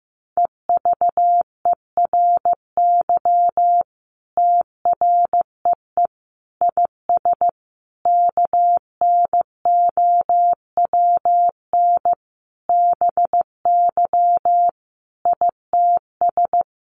Morse code, heard as a keyed tone, about 15 wpm, 700 Hz. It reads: EVERY TREE IS KNOWN BY ITS
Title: Sentences from Top 1300 Words 15wpm